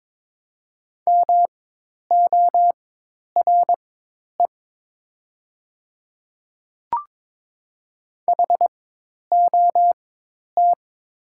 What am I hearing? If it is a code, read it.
Morse code, heard as a keyed tone, about 22 wpm, spaced as for 10 wpm, 700 Hz. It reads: MORE HOT
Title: 4x Character Spacing - Top 500 Words - 22wpm